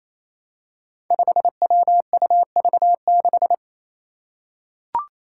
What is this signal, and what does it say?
Morse code, heard as a keyed tone, about 28 wpm, 700 Hz. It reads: 5WUV6
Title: Five Challenging Letter-Number Combinations 28wpm